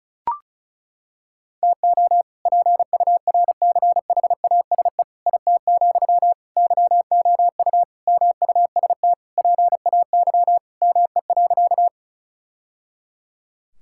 Morse code, heard as a keyed tone, about 34 wpm, 700 Hz. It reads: TO PURCHASE IT, YOU MUST PAY ME.
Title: Sentences from Top 1000 Words - Review Part 5 - 35wpm